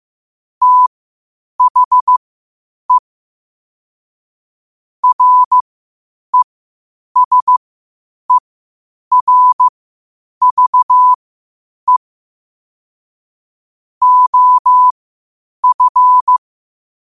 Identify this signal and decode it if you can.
Morse code, heard as a keyed tone, about 15 wpm, spaced as for 8 wpm, 1000 Hz. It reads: THE RESERVE OF
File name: qotdfiles / 8wpm